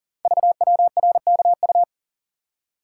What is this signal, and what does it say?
Morse code, heard as a keyed tone, about 41 wpm, 690 Hz. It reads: VWRKU
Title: Code Groups - 5 Letter:Numbers - 40wpm